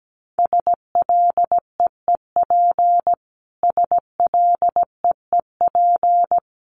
Morse code, heard as a keyed tone, about 17 wpm, 700 Hz. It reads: SLEEP SLEEP
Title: Top 500 Words Repeated 17wpm